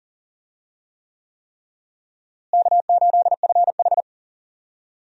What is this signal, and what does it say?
Morse code, heard as a keyed tone, about 40 wpm, 700 Hz. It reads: K8FH
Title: USA Call Signs - 4 Characters - Repeated 3x - 40wpm